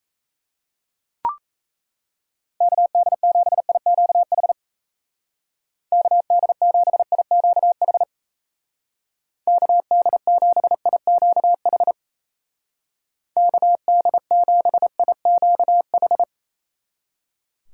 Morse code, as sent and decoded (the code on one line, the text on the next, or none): -.- -.. --... .. --.- .... / -.- -.. --... .. --.- .... / -.- -.. --... .. --.- .... / -.- -.. --... .. --.- ....
KD7IQH KD7IQH KD7IQH KD7IQH